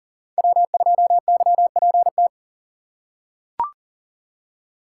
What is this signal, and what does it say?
Morse code, heard as a keyed tone, about 40 wpm, 700 Hz. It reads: W2YPT
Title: USA Call Signs - 5 Characters - 4x Character Spacing - 40wpm